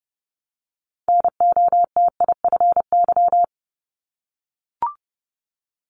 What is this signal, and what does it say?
Morse code, heard as a keyed tone, about 30 wpm, 700 Hz. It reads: NOTIFY
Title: Multi-letter Phonemes Repeated 3x 30wpm